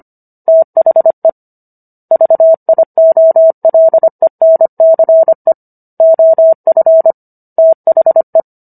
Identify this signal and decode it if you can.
Morse code, heard as a keyed tone, about 25 wpm, 650 Hz. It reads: THE VIOLENCE OF THE
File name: Book One / 16-Chapter 16 - 25 WPM